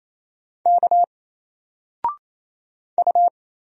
Morse code, heard as a keyed tone, about 27 wpm, 700 Hz. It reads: K U
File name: Single-Letter-Number-28wpm